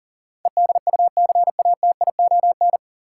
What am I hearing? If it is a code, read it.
Morse code, heard as a keyed tone, about 40 wpm, 700 Hz. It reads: EDUCATION
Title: Sentences from Top 1100 Words - New Word Drill Alt - 40wpm